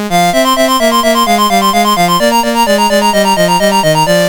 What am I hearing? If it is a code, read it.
There is no code.